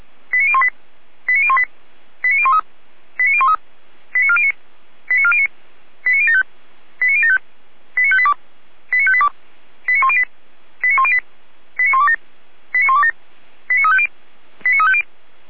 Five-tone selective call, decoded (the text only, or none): 89018 89018 89013 89013 89409 89409 89075 89075 89582 89582 89108 89108 89127 89127 89350 89350